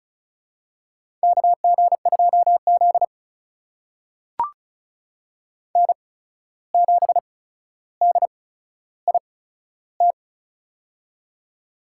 Morse code, heard as a keyed tone, about 36 wpm, spaced as for 14 wpm, 700 Hz. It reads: KG2Z N7DIT
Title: USA Call Signs - 8x Character Spacing - 35wpm